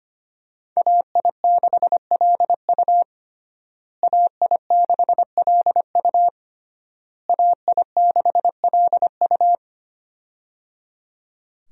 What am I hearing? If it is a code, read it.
Morse code, heard as a keyed tone, about 25 wpm, 700 Hz. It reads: AI6LU AI6LU AI6LU
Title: USA Call Signs - 5 Characters - Repeated 3x - 25wpm